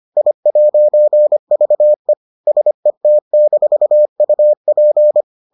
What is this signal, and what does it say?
Morse code, heard as a keyed tone, about 25 wpm, 600 Hz. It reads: I'VE SET-UP